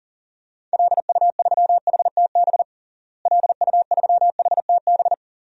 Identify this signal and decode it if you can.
Morse code, heard as a keyed tone, about 40 wpm, 700 Hz. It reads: LU3HTB LU3HTB
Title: DXCC-100 Call Signs - 6 Characters - Repeated 3x - 40wpm